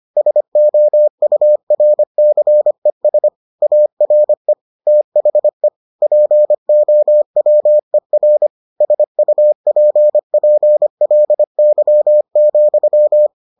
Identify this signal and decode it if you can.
Morse code, heard as a keyed tone, about 25 wpm, 600 Hz. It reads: SOURCES ARE THE POWER SUPPLY,